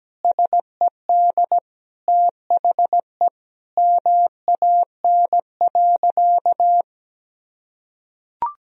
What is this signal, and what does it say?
Morse code, heard as a keyed tone, about 17 wpm, 700 Hz. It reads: SED THE MAN.
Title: Sentences from Top 200 Words and Abbreviations 17wpm